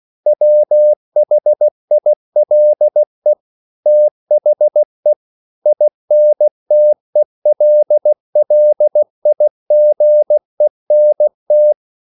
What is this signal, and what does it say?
Morse code, heard as a keyed tone, about 16 wpm, 600 Hz. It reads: WHILE THE INTELLIGENT
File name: CW / Practice / EnglishQuotations0116